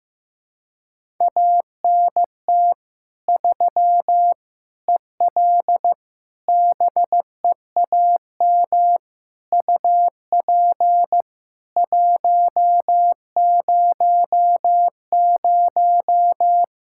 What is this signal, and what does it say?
Morse code, heard as a keyed tone, about 16 wpm, 700 Hz. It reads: ANT 3 EL BEAM UP 100
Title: K1USN SST QSO Practice 15wpm